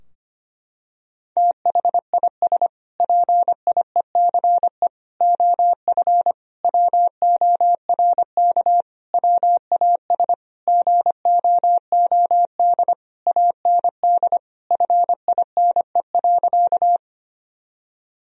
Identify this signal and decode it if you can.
Morse code, heard as a keyed tone, about 25 wpm, 700 Hz. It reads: THIS PIECE OF WORK WAS GOOD AND FINE.
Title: Sentences from Top 1000 Words - Review Part 2 - 25wpm